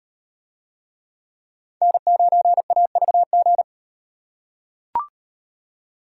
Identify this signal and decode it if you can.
Morse code, heard as a keyed tone, about 38 wpm, 700 Hz. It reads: N9AVG